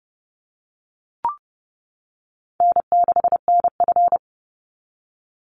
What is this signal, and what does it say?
Morse code, heard as a keyed tone, about 30 wpm, 700 Hz. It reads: N6NF
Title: Easy Call Signs - 30wpm